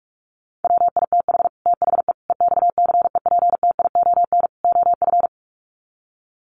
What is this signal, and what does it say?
Morse code, heard as a keyed tone, about 44 wpm, 700 Hz. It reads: WITH THE EXCEPTION OF
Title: Sets of 4 Words 45wpm